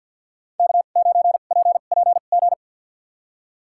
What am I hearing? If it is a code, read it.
Morse code, heard as a keyed tone, about 53 wpm, spaced as for 37 wpm, 700 Hz. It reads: K9PPG
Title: USA Call Signs - 5 Characters - 2x Character Spacing - 50wpm